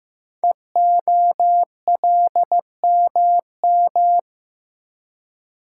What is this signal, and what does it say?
Morse code, heard as a keyed tone, about 15 wpm, 700 Hz. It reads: EOLMM
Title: Code Groups - 5 Letters - 15wpm